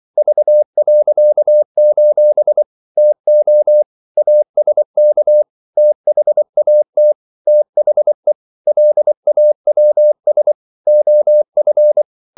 Morse code, heard as a keyed tone, about 24 wpm, 600 Hz. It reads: V.: TO ASK THAT THE LAWS OF
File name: CW / Practice / EnglishQuotations0424